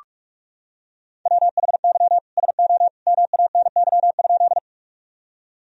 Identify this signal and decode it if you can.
Morse code, heard as a keyed tone, about 41 wpm, 700 Hz. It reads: WHY SO MANY?
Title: Sentences from Top 1000 Words - Review Part 1 - 45wpm